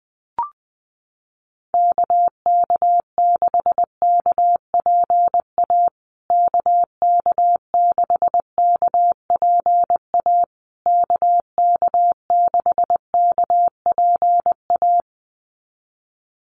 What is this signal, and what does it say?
Morse code, heard as a keyed tone, about 20 wpm, 700 Hz. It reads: KK6KPA KK6KPA KK6KPA
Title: USA Call Signs Repeated 3x 20wpm